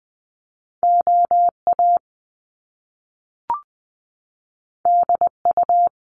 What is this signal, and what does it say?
Morse code, heard as a keyed tone, about 20 wpm, 700 Hz. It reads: OA DU
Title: Top Two Letter Combinations in Words 20wpm